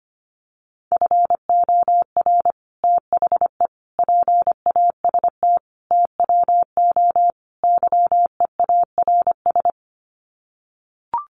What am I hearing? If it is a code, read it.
Morse code, heard as a keyed tone, about 25 wpm, 700 Hz. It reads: FOR THE PAST TWO YEARS
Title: Sets of 5 Words Encore2 25wpm